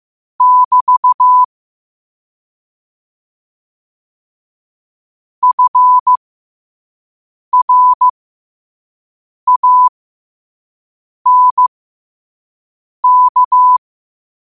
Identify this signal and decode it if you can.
Morse code, heard as a keyed tone, about 15 wpm, spaced as for 5 wpm, 1000 Hz. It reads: <BT> FRANK
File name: qotdfiles / 5wpm